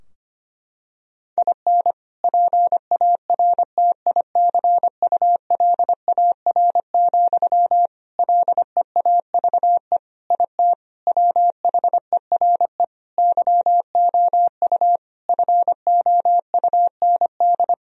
Morse code, heard as a keyed tone, about 25 wpm, 700 Hz. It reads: IN PARTICULAR, LEAVE IT WHERE YOU FOUND
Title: Sentences from Top 1000 Words - Review Part 4 - 25wpm